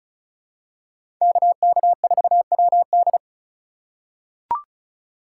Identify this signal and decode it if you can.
Morse code, heard as a keyed tone, about 36 wpm, 700 Hz. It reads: KK4WD
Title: USA Call Signs - 5 Characters - 4x Character Spacing - 35wpm